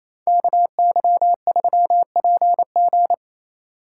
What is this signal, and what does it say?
Morse code, heard as a keyed tone, about 34 wpm, 700 Hz. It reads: KY3PG
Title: Speed Racing - USA 2x2 Call Signs 28wpm